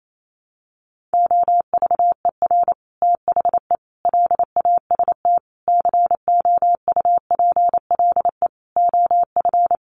Morse code, heard as a keyed tone, about 28 wpm, 700 Hz. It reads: OVER THE LAST COUPLE OF
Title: Sets of 5 Words Encore 28wpm